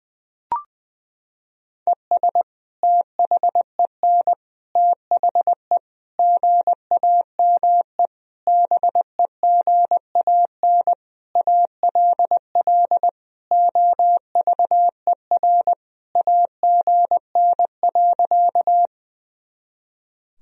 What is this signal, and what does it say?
Morse code, heard as a keyed tone, about 20 wpm, 700 Hz. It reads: ES THEN THE GAME BEGAN ALL OVER AGN.
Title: Sentences from Top 500 Words and Abbreviations 20wpm